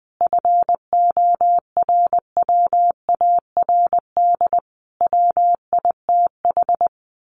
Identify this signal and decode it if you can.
Morse code, heard as a keyed tone, about 20 wpm, 700 Hz. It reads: FORWARD WITH
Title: A Sound Of Thunder 20wpm